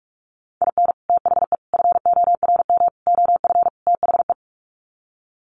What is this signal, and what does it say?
Morse code, heard as a keyed tone, about 44 wpm, 700 Hz. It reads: IN THE FORM OF THE
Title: Sets of 5 Words Encore2 45wpm